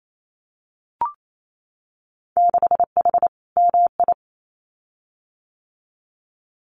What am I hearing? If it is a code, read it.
Morse code, heard as a keyed tone, about 28 wpm, 700 Hz. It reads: 6H MI
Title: Winter Field Day Exchanges Encore 28wpm